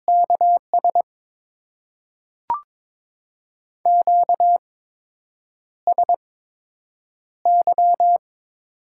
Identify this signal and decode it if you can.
Morse code, heard as a keyed tone, about 22 wpm, spaced as for 8 wpm, 700 Hz. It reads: KS QSY
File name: Top 90 QSO Elements - 8x Character Spacing - 22wpm